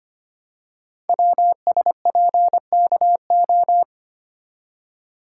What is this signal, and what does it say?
Morse code, heard as a keyed tone, about 25 wpm, 700 Hz. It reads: WSPKO
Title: Code Groups - 5 Letters - 25wpm